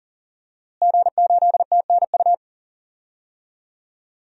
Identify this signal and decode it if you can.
Morse code, heard as a keyed tone, about 40 wpm, 700 Hz. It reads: G8TNU
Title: DXCC-100 Call Signs - 5 Characters - Repeated 3x - 40wpm